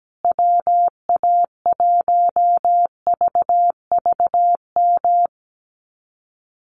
Morse code, heard as a keyed tone, about 17 wpm, 700 Hz. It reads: WA1VVM
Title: USA Call Signs Repeated 2x 17wpm